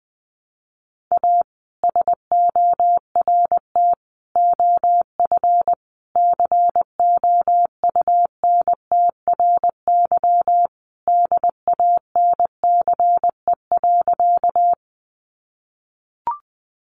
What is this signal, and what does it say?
Morse code, heard as a keyed tone, about 20 wpm, 700 Hz. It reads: A SORT OF COUNTRY DANCE.
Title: Sentences from Top 1300 Words 20wpm